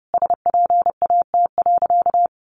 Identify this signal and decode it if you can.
Morse code, heard as a keyed tone, about 30 wpm, 700 Hz. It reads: SPAT.
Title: A Sound Of Thunder 30wpm